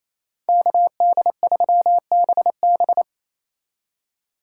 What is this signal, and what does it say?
Morse code, heard as a keyed tone, about 36 wpm, 700 Hz. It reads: KD3BB